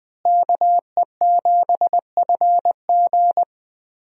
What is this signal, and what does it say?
Morse code, heard as a keyed tone, about 20 wpm, 700 Hz. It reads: KE7FG
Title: USA Call Signs - 5 Characters - Repeated 2x - 20wpm